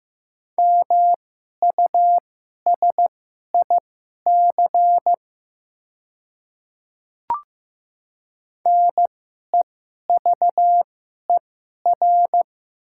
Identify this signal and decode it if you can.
Morse code, heard as a keyed tone, about 15 wpm, spaced as for 11 wpm, 700 Hz. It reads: MUSIC NEVER
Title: 2x Character Spacing - Top 300 Words - 15wpm